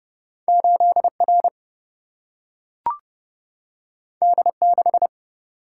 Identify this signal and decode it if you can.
Morse code, heard as a keyed tone, about 30 wpm, 700 Hz. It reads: 8 R D 6
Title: Single-Letter Single-Number 30wpm